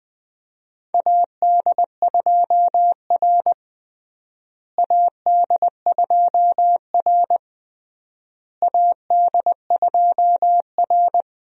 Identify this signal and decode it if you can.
Morse code, heard as a keyed tone, about 21 wpm, 700 Hz. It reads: AD2R AD2R AD2R
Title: USA Call Signs - 4 Characters - Repeated 3x - 20wpm